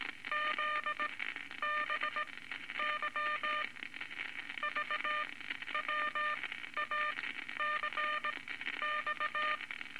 Morse code, heard as a keyed tone, about 18 wpm, spaced as for 11 wpm, 1210 Hz, with noise. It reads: ZBY VWACX
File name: Morse Code Records / 19 A LCM 166 Ten Words Per Minute